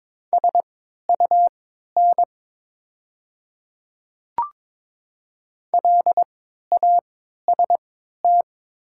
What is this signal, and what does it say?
Morse code, heard as a keyed tone, about 22 wpm, spaced as for 12 wpm, 700 Hz. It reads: SUN LAST
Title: 3x Character Spacing - Top 300 Words - 22wpm